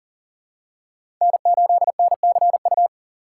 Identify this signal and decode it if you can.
Morse code, heard as a keyed tone, about 40 wpm, 700 Hz. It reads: N8NCU
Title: USA Call Signs - 5 Characters - 3x Character Spacing - 40wpm